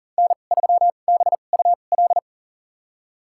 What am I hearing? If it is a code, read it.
Morse code, heard as a keyed tone, about 40 wpm, spaced as for 29 wpm, 700 Hz. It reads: N3BUL
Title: USA Call Signs - 2x Character Spacing - 40wpm